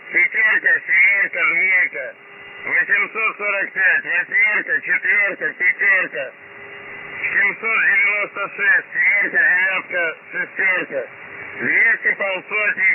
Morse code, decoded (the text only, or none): none